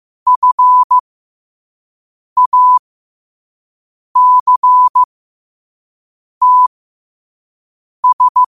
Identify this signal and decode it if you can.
Morse code, heard as a keyed tone, about 15 wpm, spaced as for 5 wpm, 1000 Hz. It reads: FACTS
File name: qotdfiles / 5wpm